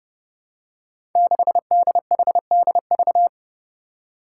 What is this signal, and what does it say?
Morse code, heard as a keyed tone, about 30 wpm, 700 Hz. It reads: 6DHDV
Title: Five Challenging Letter-Number Combinations 30wpm